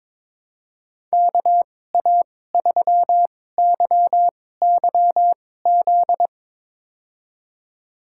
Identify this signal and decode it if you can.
Morse code, heard as a keyed tone, about 22 wpm, spaced as for 16 wpm, 700 Hz. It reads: KA3YYZ
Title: USA Call Signs - 6 Characters - 2x Character Spacing - 22wpm